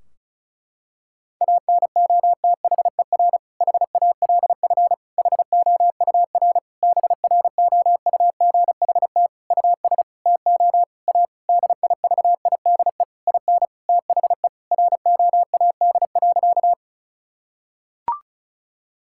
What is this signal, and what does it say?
Morse code, heard as a keyed tone, about 36 wpm, 700 Hz. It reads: ANOTHER HALF HOUR BROUGHT US TO A DIVIDE IN THE ROAD.
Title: Sentences from Top 1600 Words 35wpm